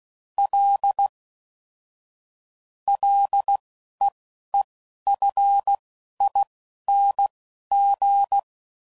Morse code, heard as a keyed tone, about 16 wpm, spaced as for 11 wpm, 800 Hz. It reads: L LEEFING
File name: Prefixes / FLs & THs / L Prefixes